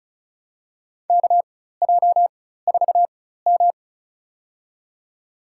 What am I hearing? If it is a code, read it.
Morse code, heard as a keyed tone, about 36 wpm, spaced as for 16 wpm, 700 Hz. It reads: KJ4M
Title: USA Call Signs - 4 Characters - 4x Character Spacing - 35wpm